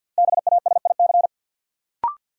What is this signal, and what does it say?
Morse code, heard as a keyed tone, about 47 wpm, 700 Hz. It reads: BASIC
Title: Sentences from Top 1000 Words - Review Part 3 - New Word Drill - 50wpm